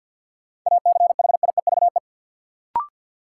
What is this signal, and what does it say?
Morse code, heard as a keyed tone, about 52 wpm, 700 Hz. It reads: ACHIEVE
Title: Sentences from Top 1000 Words - Review Part 5 - New Word Drill - 50wpm